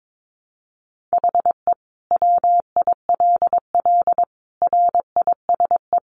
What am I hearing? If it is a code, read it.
Morse code, heard as a keyed tone, about 22 wpm, 700 Hz. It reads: HE WILL RISE
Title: Sentences from Top 900 Words 22wpm